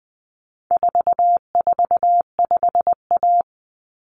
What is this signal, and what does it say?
Morse code, heard as a keyed tone, about 20 wpm, 700 Hz. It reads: 445A
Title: Common Cut Numbers - 4-Digit 20wpm